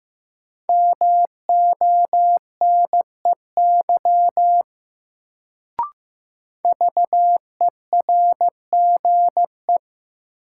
Morse code, heard as a keyed tone, about 15 wpm, 700 Hz. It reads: MONEY VERGE
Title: Sets of 4 Words - New Word Drill - 15wpm